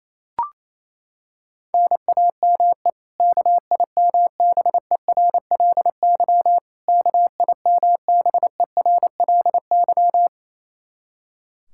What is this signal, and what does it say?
Morse code, heard as a keyed tone, about 28 wpm, 700 Hz. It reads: NAME KIMBERLY KIMBERLY
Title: Practice QSO - Exchange 1 - Beginning - 28wpm